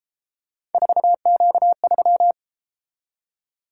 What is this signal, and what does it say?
Morse code, heard as a keyed tone, about 32 wpm, 700 Hz. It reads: 4Q3